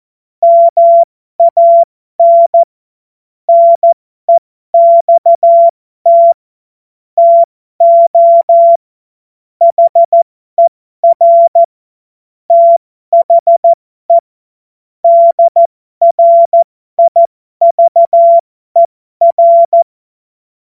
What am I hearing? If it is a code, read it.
Morse code, heard as a keyed tone, about 14 wpm, spaced as for 12 wpm, 680 Hz. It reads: MAN NEXT TO HER THE DRIVER
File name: Morsekurs / Witze